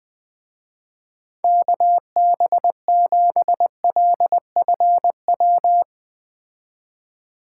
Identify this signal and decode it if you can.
Morse code, heard as a keyed tone, about 20 wpm, 700 Hz. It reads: KB7LFW
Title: Speed Racing - USA 2x3 Call Signs 17wpm